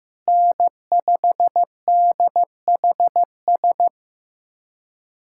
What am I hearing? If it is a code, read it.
Morse code, heard as a keyed tone, about 15 wpm, 700 Hz. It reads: N5DHS
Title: USA Callsign DE Callsign - 5 Characters - 15wpm